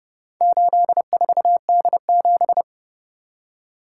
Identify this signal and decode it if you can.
Morse code, heard as a keyed tone, about 29 wpm, 700 Hz. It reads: 84D7